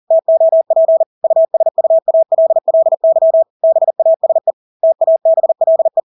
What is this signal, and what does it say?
Morse code, heard as a keyed tone, about 40 wpm, 650 Hz. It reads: TOP USUALLY BASE TABLE
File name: Engelske ord 40-40 wpm